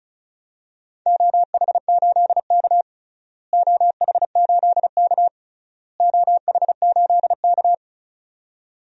Morse code, heard as a keyed tone, about 36 wpm, 700 Hz. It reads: OH8K OH8K OH8K
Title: DXCC-100 Call Signs - 4 Characters - Repeated 3x - 35wpm